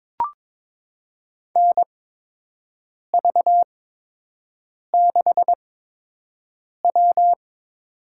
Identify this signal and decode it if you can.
Morse code, heard as a keyed tone, about 22 wpm, spaced as for 6 wpm, 700 Hz. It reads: NV6W
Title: USA Call Signs - 8x Character Spacing - 22wpm